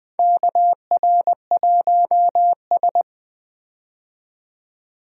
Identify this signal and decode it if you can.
Morse code, heard as a keyed tone, about 21 wpm, 700 Hz. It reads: KR1S
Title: USA Call Signs - 4 Characters - 20wpm